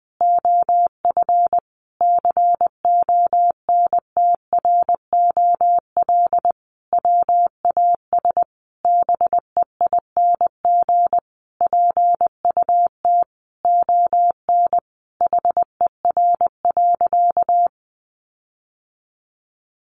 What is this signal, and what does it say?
Morse code, heard as a keyed tone, about 20 wpm, 700 Hz. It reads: OF CONTROL WAS BEING PUT ON HER.
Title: Sentences from Top 1300 Words 20wpm